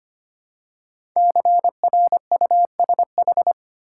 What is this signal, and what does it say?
Morse code, heard as a keyed tone, about 25 wpm, 700 Hz. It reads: CRUSH
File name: 5 Letter Words - First 3 Letters Given - 25wpm